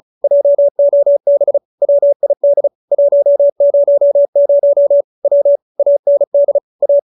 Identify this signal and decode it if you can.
Morse code, heard as a keyed tone, about 34 wpm, 570 Hz. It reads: JOB WID 100 W AND A
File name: QSO313 35